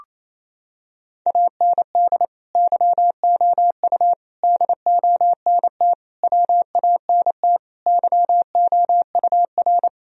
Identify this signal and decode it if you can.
Morse code, heard as a keyed tone, about 28 wpm, 700 Hz. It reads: AND YOU DONT WANT YOUR
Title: Sentences from Top 1000 Words - Review Part 2 - 28wpm